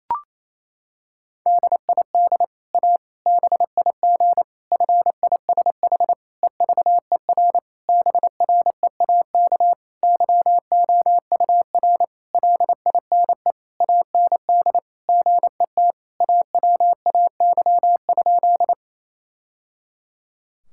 Morse code, heard as a keyed tone, about 28 wpm, 700 Hz. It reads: DID A BIG FISH EVER BREAK YOUR LINE AND GET AWAY?
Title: Sentences from Top 1000 Words - Review Part 6 - 28wpm